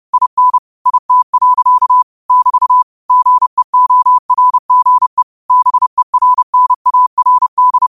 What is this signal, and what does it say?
Morse code, heard as a keyed tone, about 30 wpm, 1000 Hz. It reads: IN IT. <BT> GEORGE BERNARD